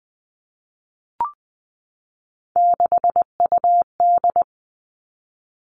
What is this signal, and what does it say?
Morse code, heard as a keyed tone, about 20 wpm, 700 Hz. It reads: 6UD
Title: Three Challenging Letter-Number Combinations 20wpm